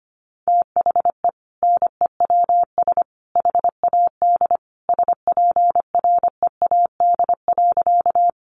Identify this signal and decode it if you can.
Morse code, heard as a keyed tone, about 25 wpm, 700 Hz. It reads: THE NEWS HAD SPREAD.
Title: Sentences from Top 1200 Words 25wpm